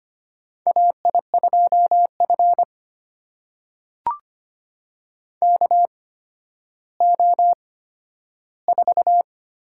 Morse code, heard as a keyed tone, about 25 wpm, spaced as for 12 wpm, 700 Hz. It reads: AI2F KO4
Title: USA Call Signs - 8x Character Spacing - 25wpm